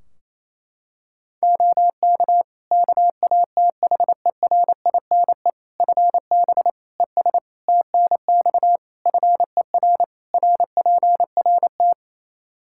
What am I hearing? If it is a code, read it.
Morse code, heard as a keyed tone, about 28 wpm, 700 Hz. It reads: OK KATHERINE FB ES TNX FER RPRT
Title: K1USN SST QSO Practice 28wpm